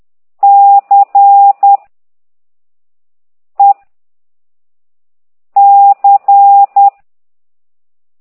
Morse code, heard as a keyed tone, about 10 wpm, 800 Hz. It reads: C E C
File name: exerc7